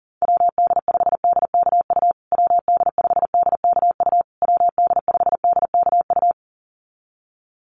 Morse code, heard as a keyed tone, about 40 wpm, 700 Hz. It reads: WD5DKU WD5DKU WD5DKU
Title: USA Call Signs Repeated 3x 40wpm